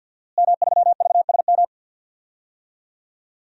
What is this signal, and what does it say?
Morse code, heard as a keyed tone, about 50 wpm, 700 Hz. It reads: M3VSM
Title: DXCC-100 Call Signs - 5 Characters - Repeated 3x - 50wpm